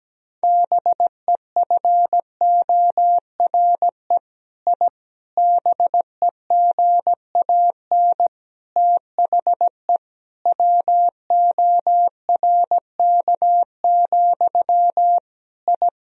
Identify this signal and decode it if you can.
Morse code, heard as a keyed tone, about 17 wpm, 700 Hz. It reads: BEFORE I BEGAN THE WORK, I
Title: Sentences from Top 1000 Words - Review Part 4 - 17wpm